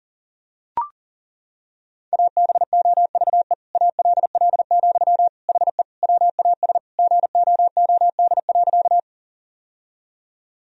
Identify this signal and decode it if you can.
Morse code, heard as a keyed tone, about 40 wpm, 700 Hz. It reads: ABOVE ALL, HE WAS GOOD.
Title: Sentences from Top 1000 Words - Review Part 2 - 40wpm